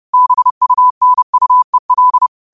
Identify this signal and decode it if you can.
Morse code, heard as a keyed tone, about 30 wpm, 1000 Hz. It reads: BUNUEL
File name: qotdfiles / 30wpm